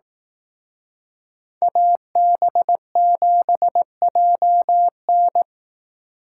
Morse code, heard as a keyed tone, about 18 wpm, 700 Hz. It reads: AB7JN
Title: Speed Racing - USA 2x2 Call Signs 15wpm